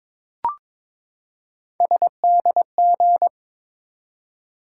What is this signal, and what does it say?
Morse code, heard as a keyed tone, about 22 wpm, 700 Hz. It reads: SDG
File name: ARRL Sections v2 - 22wpm